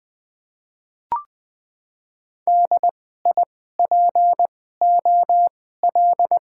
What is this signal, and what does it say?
Morse code, heard as a keyed tone, about 20 wpm, 700 Hz. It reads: D I P O L
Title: Top 90 QSO Elements - 2x Character Spacing - 20wpm